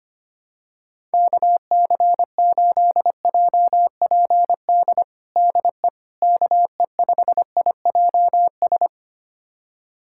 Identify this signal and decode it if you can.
Morse code, heard as a keyed tone, about 25 wpm, 700 Hz. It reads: KC8JPD DE KE5IJS
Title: USA Callsign DE Callsign - 6 Characters - 25wpm